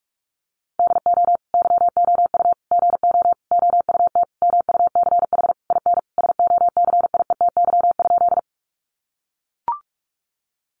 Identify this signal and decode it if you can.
Morse code, heard as a keyed tone, about 44 wpm, 700 Hz. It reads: DO YOU GO OUT MUCH IN SOCIETY?
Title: Sentences from Top 700 Words 45wpm